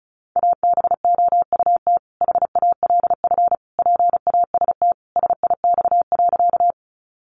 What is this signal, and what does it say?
Morse code, heard as a keyed tone, about 34 wpm, 700 Hz. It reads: ABOUT HALF PAST SIX.
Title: Sentences from Top 600 Words 35wpm